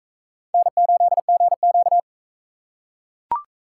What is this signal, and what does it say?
Morse code, heard as a keyed tone, about 42 wpm, 700 Hz. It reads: N8GQ